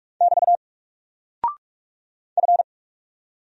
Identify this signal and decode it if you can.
Morse code, heard as a keyed tone, about 44 wpm, 700 Hz. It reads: <BT> F
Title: Single Character (Letter:Number:Punctuation:Prosign) Alt 45wpm